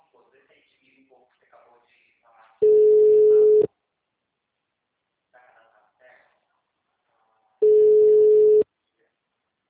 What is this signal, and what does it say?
Ringback tone (Europe, 425 Hz), with speech.
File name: out- -2044- - - .44387